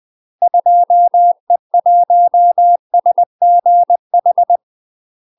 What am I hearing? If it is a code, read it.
Morse code, heard as a keyed tone, about 20 wpm, 700 Hz. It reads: 2E1SGH
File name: UK MASTER QSOs TRAINING / 20WPM / UK MASTER QSOs 0051